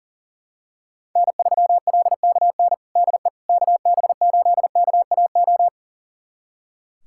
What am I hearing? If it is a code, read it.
Morse code, heard as a keyed tone, about 40 wpm, 700 Hz. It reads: N3LKN DE KB8KAO
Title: USA Callsign DE Callsign 40wpm